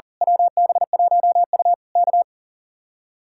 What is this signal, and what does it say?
Morse code, heard as a keyed tone, about 38 wpm, 700 Hz. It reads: WB1U K
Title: Speed-Racing Common QSO Phrases - 40wpm